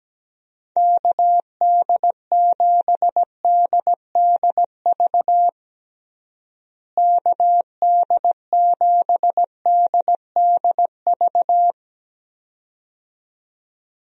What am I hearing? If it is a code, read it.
Morse code, heard as a keyed tone, about 17 wpm, 700 Hz. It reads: KD7DDV KD7DDV
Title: USA Call Signs - 6 Characters - Repeated 2x - 17wpm